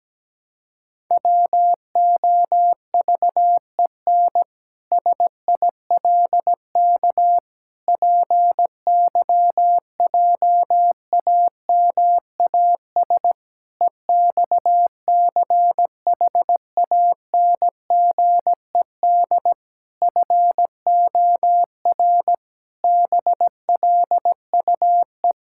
Morse code, heard as a keyed tone, about 17 wpm, 700 Hz. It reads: WOVEN SILK PYJAMAS EXCHANGED FOR BLUE
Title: Pangram Challenge v2 - 17wpm